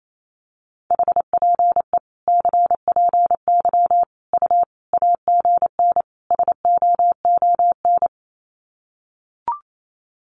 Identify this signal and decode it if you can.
Morse code, heard as a keyed tone, about 28 wpm, 700 Hz. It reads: HPE CPY U AGN SOON
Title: Common QSO Phrases 28wpm